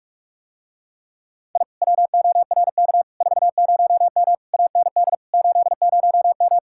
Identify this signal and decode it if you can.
Morse code, heard as a keyed tone, about 49 wpm, 700 Hz. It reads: I WORK 40M AND 80M